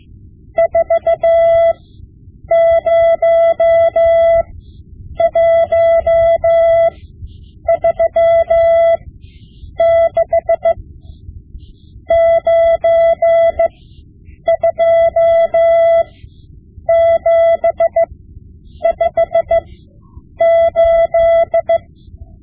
Morse code, heard as a keyed tone, about 15 wpm, spaced as for 8 wpm, 670 Hz. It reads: 40136 92758
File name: Morse Lessons / Morse Lessons / Tape-5B